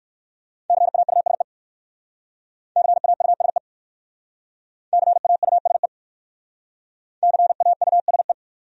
Morse code, heard as a keyed tone, about 57 wpm, 700 Hz. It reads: CAUSE CAUSE CAUSE CAUSE